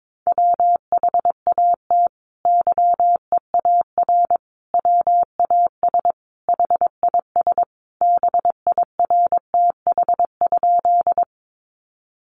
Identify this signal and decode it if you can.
Morse code, heard as a keyed tone, about 22 wpm, 700 Hz. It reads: WHAT YEAR WAS HIS BIRTH?
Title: Sentences from Top 800 Words 22wpm